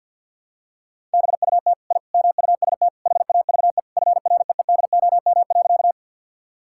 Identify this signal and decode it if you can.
Morse code, heard as a keyed tone, about 52 wpm, 700 Hz. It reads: BUT I MUST HAVE FREEDOM.
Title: Sentences from Top 1000 Words - Review Part 5 - 50wpm